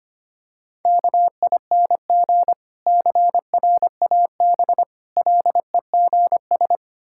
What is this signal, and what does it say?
Morse code, heard as a keyed tone, about 25 wpm, 700 Hz. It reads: KING CRAB LEGS
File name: practice / 25wpm / Difficult Words with Hints - 25wpm